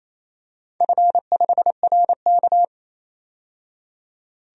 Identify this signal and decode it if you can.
Morse code, heard as a keyed tone, about 28 wpm, 700 Hz. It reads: F5RK
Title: DXCC-100 Call Signs - 4 Characters - 28wpm